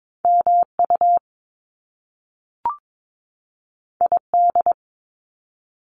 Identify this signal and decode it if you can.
Morse code, heard as a keyed tone, about 22 wpm, 700 Hz. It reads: MU ID
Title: Top Two Letter Combinations in Words 22wpm